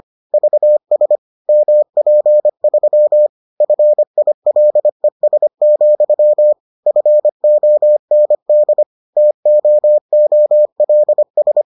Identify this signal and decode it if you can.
Morse code, heard as a keyed tone, about 25 wpm, 600 Hz. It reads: VS MP3 FILES, FOND TOOLS